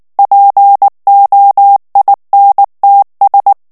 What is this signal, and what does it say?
Morse code, heard as a keyed tone, about 19 wpm, 800 Hz. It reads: POINTS
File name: ANGLAIS texte clair 3 vitesse 18 mots minute